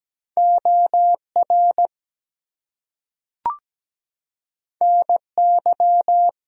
Morse code, heard as a keyed tone, about 17 wpm, 700 Hz. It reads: OR NY
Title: US States and Canadian Provinces 17wpm